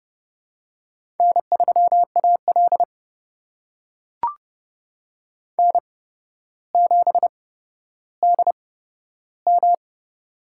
Morse code, heard as a keyed tone, about 30 wpm, spaced as for 13 wpm, 700 Hz. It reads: N3AL N7DM
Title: USA Call Signs - 4 Characters - 8x Character Spacing - 30wpm